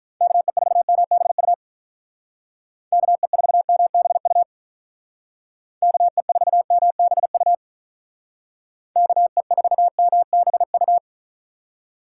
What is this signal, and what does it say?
Morse code, heard as a keyed tone, about 43 wpm, 700 Hz. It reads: KE4MBU KE4MBU KE4MBU KE4MBU
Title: Speed Racing - USA 2x3 Call Signs 35wpm